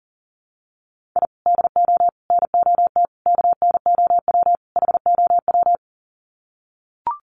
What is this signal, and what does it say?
Morse code, heard as a keyed tone, about 40 wpm, 700 Hz. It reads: I DO NOT KNOW HOW
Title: Sets of 5 Words 40wpm